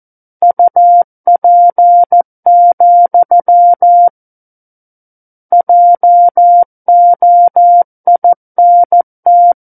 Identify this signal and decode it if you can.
Morse code, heard as a keyed tone, about 14 wpm, 700 Hz. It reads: UP, JOINT
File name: Book One / 05-Chapter 5 - 14 WPM